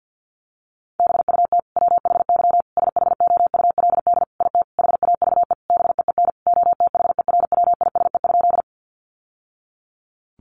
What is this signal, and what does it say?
Morse code, heard as a keyed tone, about 50 wpm, 700 Hz. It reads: BUT WHY SHOULD IT HAVE BEEN OTHERWISE?
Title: Sentences from Top 1200 Words 50wpm